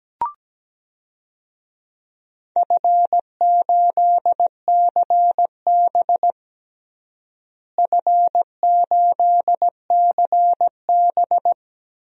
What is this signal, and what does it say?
Morse code, heard as a keyed tone, about 17 wpm, 700 Hz. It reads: F8CB F8CB
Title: DXCC-100 Call Signs - 4 Characters - Repeated 2x - 17wpm